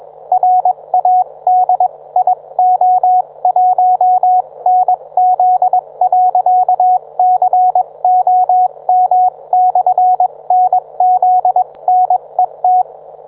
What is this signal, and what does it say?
Morse code, heard as a keyed tone, about 21 wpm, spaced as for 19 wpm, 730 Hz, with radio noise. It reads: RADIO1NZ.COM/NZNET
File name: book